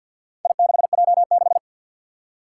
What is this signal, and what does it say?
Morse code, heard as a keyed tone, about 47 wpm, 700 Hz. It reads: I6J6
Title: Four Challenging Letter-Number Combinations 50wpm